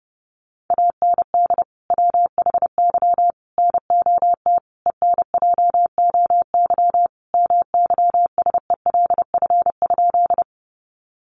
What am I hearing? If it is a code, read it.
Morse code, heard as a keyed tone, about 30 wpm, 700 Hz. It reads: AND WHY NOT ENJOY MYSELF?
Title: Sentences from Top 1300 Words 30wpm